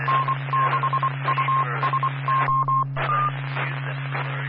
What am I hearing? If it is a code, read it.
Morse code, heard as a keyed tone, about 24 wpm, 1060 Hz. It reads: N6USO
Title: Claremont, CA Police, Amateur Radio